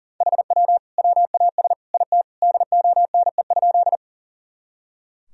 Morse code, heard as a keyed tone, about 40 wpm, 700 Hz. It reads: HW WAS IT DONE?
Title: Sentences from Top 500 Words and Abbreviations 40wpm